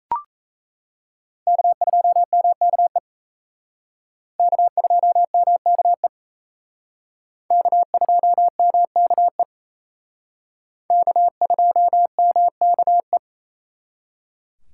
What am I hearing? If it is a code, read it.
Morse code, heard as a keyed tone, about 35 wpm, 700 Hz. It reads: K2MKE K2MKE K2MKE K2MKE
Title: Speed Racing - USA 1x3 Call Signs 28wpm